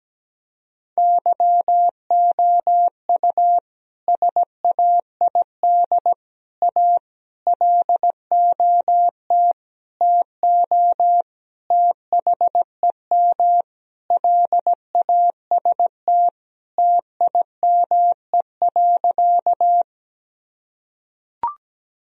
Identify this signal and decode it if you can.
Morse code, heard as a keyed tone, about 17 wpm, 700 Hz. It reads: YOU SAID A LOT TO THEM LAST TIME.
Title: Sentences from Top 1000 Words - Review Part 3 - 17wpm